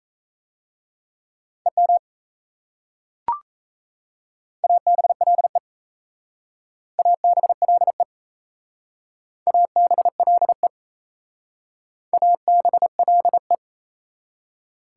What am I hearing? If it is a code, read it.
Morse code, heard as a keyed tone, about 35 wpm, 690 Hz. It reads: EM ABLE ABLE ABLE ABLE
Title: Speed-Racing Prefixes and Suffixes 28wpm